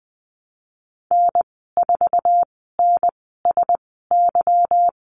Morse code, heard as a keyed tone, about 20 wpm, spaced as for 15 wpm, 700 Hz. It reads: N4NSY
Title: 173 Lesson 28g - Number 9 - Call Signs 20wpm